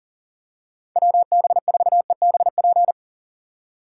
Morse code, heard as a keyed tone, about 40 wpm, 700 Hz. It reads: WB4EBP
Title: USA Call Signs - 2x Character Spacing - 40wpm